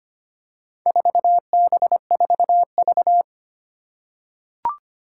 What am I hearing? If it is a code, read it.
Morse code, heard as a keyed tone, about 25 wpm, 700 Hz. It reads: 4B4V
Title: Four Challenging Letter-Number Combinations 25wpm